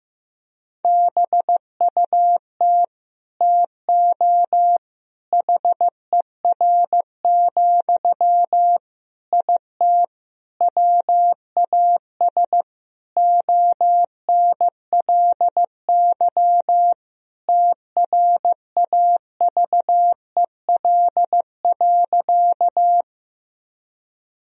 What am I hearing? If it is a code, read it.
Morse code, heard as a keyed tone, about 15 wpm, 700 Hz. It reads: BUT TO HER, IT WAS ONLY TRAVEL.
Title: Sentences from Top 1000 Words - Review Part 2 - 15wpm